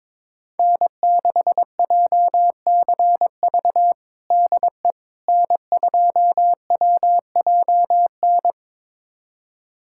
Morse code, heard as a keyed tone, about 22 wpm, 700 Hz. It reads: N6JCV DE N2WJN
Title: USA Callsign DE Callsign - 5 Characters - 22wpm